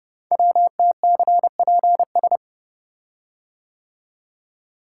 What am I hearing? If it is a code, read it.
Morse code, heard as a keyed tone, about 30 wpm, 700 Hz. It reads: WTCPS
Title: Code Groups - 5 Letters - 30wpm